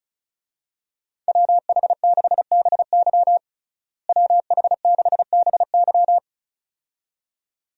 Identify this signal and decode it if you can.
Morse code, heard as a keyed tone, about 36 wpm, 700 Hz. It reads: WH6BY WH6BY
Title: USA Call Signs - 5 Characters - Repeated 2x - 35wpm